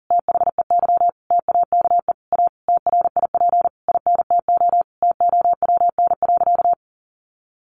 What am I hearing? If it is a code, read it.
Morse code, heard as a keyed tone, about 40 wpm, 700 Hz. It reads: THEY TAKE A TRIP INTO TOWN.
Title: Sentences from Top 1300 Words 40wpm